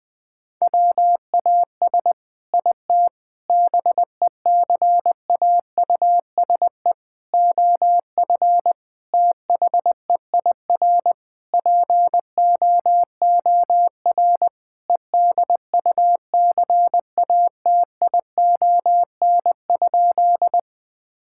Morse code, heard as a keyed tone, about 20 wpm, 700 Hz. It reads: WAS IT BECAUSE OF THEIR POOR EDUCATION?
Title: Sentences from Top 1000 Words - Review Part 6 - 20wpm